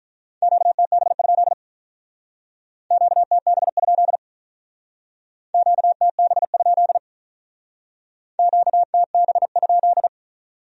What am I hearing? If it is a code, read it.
Morse code, heard as a keyed tone, about 44 wpm, 700 Hz. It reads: QTB? QTB? QTB? QTB?